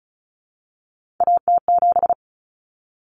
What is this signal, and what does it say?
Morse code, heard as a keyed tone, about 34 wpm, 700 Hz. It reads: AT7
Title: Common Cut Numbers - 3-Digit 35wpm